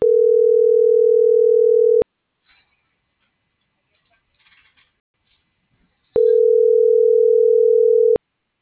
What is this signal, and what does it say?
Ringback tone (North America), 440 + 480 Hz.